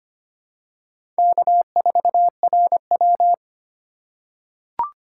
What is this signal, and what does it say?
Morse code, heard as a keyed tone, about 25 wpm, 700 Hz. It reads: K4RW
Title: USA Call Signs - 4 Characters - 3x Character Spacing - 25wpm